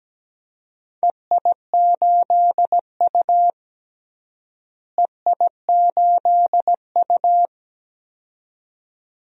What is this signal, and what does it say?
Morse code, heard as a keyed tone, about 17 wpm, 700 Hz. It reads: EI8U EI8U
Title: DXCC-100 Call Signs - 4 Characters - Repeated 2x - 17wpm